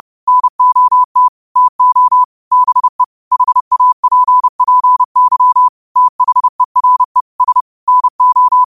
Morse code, heard as a keyed tone, about 30 wpm, 1000 Hz. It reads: NOT TO BE HAPPY THERE S NO
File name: qotdfiles / 30wpm